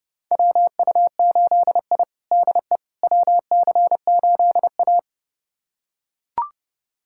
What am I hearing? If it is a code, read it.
Morse code, heard as a keyed tone, about 30 wpm, 700 Hz. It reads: WU8I DE WC8A
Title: USA Callsign DE Callsign - 4 Characters - 30wpm